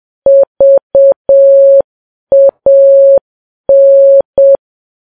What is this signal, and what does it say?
Morse code, heard as a keyed tone, about 7 wpm, 550 Hz. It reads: VAN
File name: Files Web tls / Morse on9fv / morse7kps